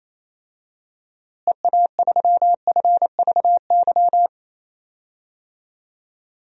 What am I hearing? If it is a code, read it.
Morse code, heard as a keyed tone, about 27 wpm, 700 Hz. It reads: EA3FVY
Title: DXCC-100 Call Signs - 6 Characters - Repeated 3x - 28wpm